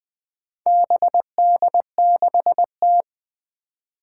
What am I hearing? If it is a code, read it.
Morse code, heard as a keyed tone, about 20 wpm, 700 Hz. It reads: BD6T
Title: Four Challenging Letter-Number Combinations 20wpm